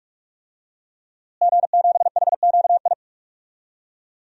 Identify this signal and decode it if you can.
Morse code, heard as a keyed tone, about 42 wpm, 700 Hz. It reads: G7HQI
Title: DXCC-100 Call Signs - 5 Characters - Repeated 2x - 45wpm